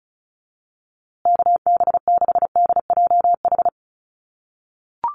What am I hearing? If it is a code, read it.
Morse code, heard as a keyed tone, about 34 wpm, 700 Hz. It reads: KB6DJH
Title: USA Call Signs Repeated 3x 35wpm